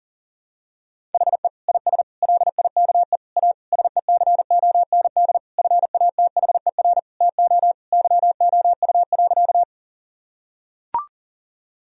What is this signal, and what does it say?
Morse code, heard as a keyed tone, about 40 wpm, 700 Hz. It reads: HE IS LIKE A SECOND FATHER TO YOU.
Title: Sentences from Top 1000 Words - Review Part 1 - 40wpm